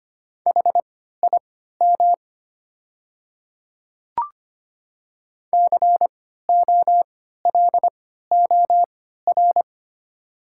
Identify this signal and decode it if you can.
Morse code, heard as a keyed tone, about 25 wpm, spaced as for 14 wpm, 700 Hz. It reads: HIM COLOR
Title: 3x Character Spacing - Top 300 Words - 25wpm